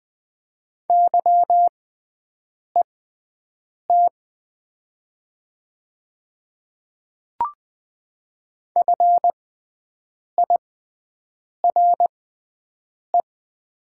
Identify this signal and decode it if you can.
Morse code, heard as a keyed tone, about 20 wpm, spaced as for 7 wpm, 690 Hz. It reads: YET FIRE
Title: 6x Character Spacing - Top 500 Words - 20wpm